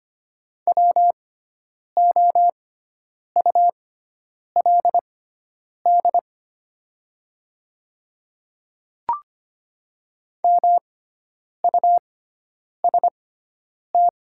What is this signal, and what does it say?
Morse code, heard as a keyed tone, about 25 wpm, spaced as for 9 wpm, 700 Hz. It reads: WOULD MUST
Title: 6x Character Spacing - Top 300 Words - 25wpm